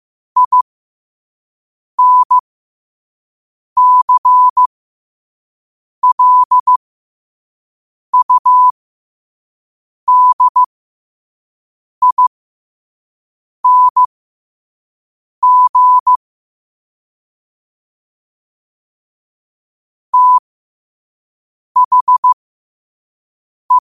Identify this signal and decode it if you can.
Morse code, heard as a keyed tone, about 15 wpm, spaced as for 5 wpm, 1000 Hz. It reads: INCLUDING THE
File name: qotdfiles / 5wpm